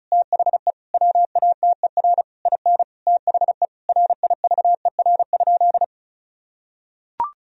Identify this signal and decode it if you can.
Morse code, heard as a keyed tone, about 36 wpm, 700 Hz. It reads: THE WATER IN THE RIVER?
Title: Sentences from Top 1000 Words - Review Part 4 - 35wpm